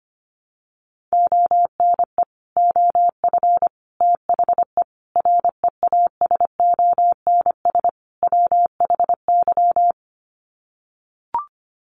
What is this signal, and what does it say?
Morse code, heard as a keyed tone, about 25 wpm, 700 Hz. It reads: ONE OF THE REASONS WHY
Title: Sets of 5 Words 25wpm